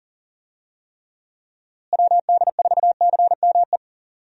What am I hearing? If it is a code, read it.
Morse code, heard as a keyed tone, about 37 wpm, 700 Hz. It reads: WD4CME
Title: USA Call Signs - 6 Characters - Repeated 3x - 40wpm